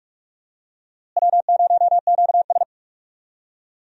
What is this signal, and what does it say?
Morse code, heard as a keyed tone, about 47 wpm, 700 Hz. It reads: W0QS
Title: USA Call Signs - 4 Characters - Repeated 3x - 45wpm